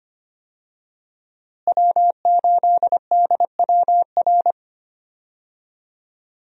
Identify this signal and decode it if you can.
Morse code, heard as a keyed tone, about 25 wpm, 700 Hz. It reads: W8DWR
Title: USA Call Signs - 5 Characters - Repeated 3x - 25wpm